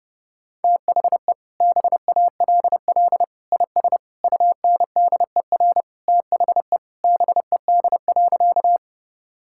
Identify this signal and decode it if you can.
Morse code, heard as a keyed tone, about 30 wpm, 700 Hz. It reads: THE BALL IS UNDER THE BED.
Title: Sentences from Top 1000 Words - Review Part 6 - 30wpm